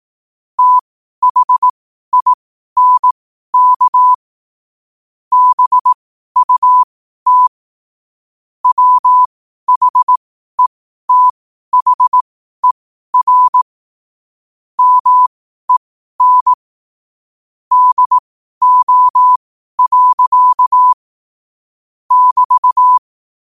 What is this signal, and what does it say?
Morse code, heard as a keyed tone, about 18 wpm, spaced as for 12 wpm, 1000 Hz. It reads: THINK BUT WHETHER MEN DO. <BT>
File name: qotdfiles / 12wpm